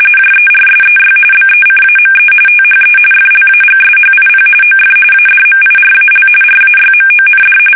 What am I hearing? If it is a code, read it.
RTTY, about 45.45 baud. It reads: JERSEY BECAME THE FLAGSHIP OF ADMIRAL RAYMOND
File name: fsk850 / battleshipnj